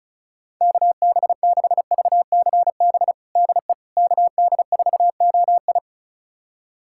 Morse code, heard as a keyed tone, about 36 wpm, 700 Hz. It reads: KB6VCB DE KD4OI
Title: USA Callsign DE Callsign 35wpm